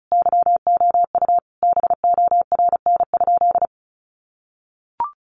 Morse code, heard as a keyed tone, about 34 wpm, 700 Hz. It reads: YOU BORN?
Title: Sentences from Top 1000 Words 35wpm